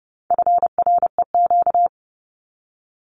Morse code, heard as a keyed tone, about 30 wpm, 700 Hz. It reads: FREQ
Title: Abbreviations 30wpm